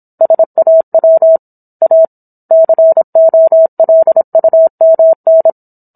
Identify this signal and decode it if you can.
Morse code, heard as a keyed tone, about 27 wpm, 650 Hz. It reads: SAW A COLUMN